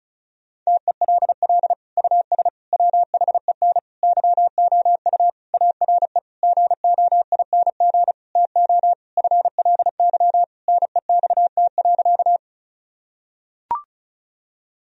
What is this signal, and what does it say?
Morse code, heard as a keyed tone, about 34 wpm, 700 Hz. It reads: TELL US WHEN YOU ARE GOING TO FLY NEXT.
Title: Sentences from Top 1000 Words - Review Part 2 - 35wpm